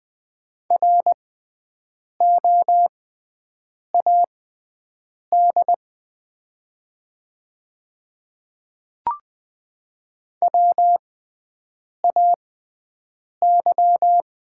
Morse code, heard as a keyed tone, about 20 wpm, spaced as for 7 wpm, 700 Hz. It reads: ROAD WAY
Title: 6x Character Spacing - Top 500 Words - 20wpm